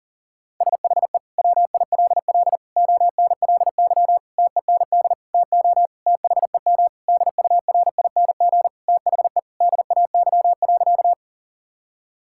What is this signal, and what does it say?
Morse code, heard as a keyed tone, about 40 wpm, 700 Hz. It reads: SHE WILL ONLY TEND TO THEM DURING THE DAY.
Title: Sentences from Top 1000 Words - Review Part 5 - 40wpm